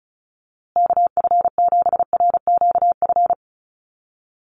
Morse code, heard as a keyed tone, about 34 wpm, 700 Hz. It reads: KF7RQF
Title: USA Call Signs Repeated 2x 35wpm